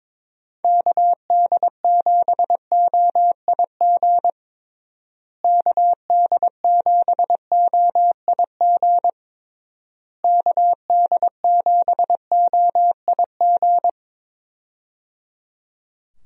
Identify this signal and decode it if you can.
Morse code, heard as a keyed tone, about 22 wpm, 700 Hz. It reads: KD7OIG KD7OIG KD7OIG
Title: USA Call Signs - 6 Characters - Repeated 3x - 22wpm